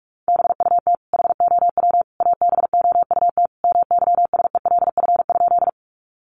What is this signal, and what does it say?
Morse code, heard as a keyed tone, about 44 wpm, 700 Hz. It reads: BUT HOW ABOUT MYSELF?
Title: Sentences from Top 1300 Words 45wpm